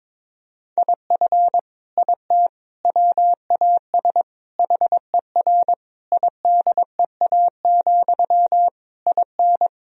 Morse code, heard as a keyed tone, about 22 wpm, 700 Hz. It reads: IF IT WAS HER IDEA, IN
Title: Sentences from Top 1000 Words - Review Part 2 - 22wpm